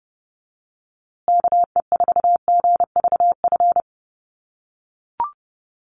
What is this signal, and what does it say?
Morse code, heard as a keyed tone, about 30 wpm, 700 Hz. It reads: KE4GVF
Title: USA Call Signs Repeated 3x 30wpm